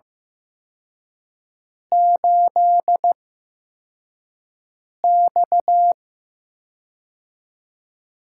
Morse code, heard as a keyed tone, about 15 wpm, spaced as for 4 wpm, 700 Hz. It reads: 8X
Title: USA Call Signs - 4 Characters - 8x Character Spacing - 15wpm